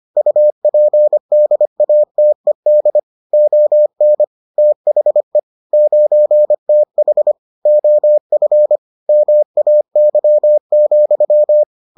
Morse code, heard as a keyed tone, about 25 wpm, 600 Hz. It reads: UPDATED ON THE 9TH OF MAY,